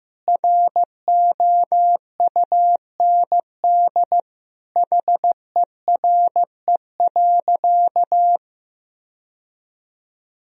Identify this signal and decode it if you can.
Morse code, heard as a keyed tone, about 15 wpm, 700 Hz. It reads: ROUND HERE.
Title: Sentences from Top 1000 Words - Review Part 1 - 15wpm